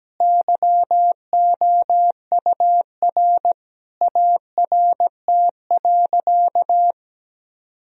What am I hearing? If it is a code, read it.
Morse code, heard as a keyed tone, about 17 wpm, 700 Hz. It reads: YOUR ART.
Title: Sentences from Top 1000 Words - Review Part 3 - 17wpm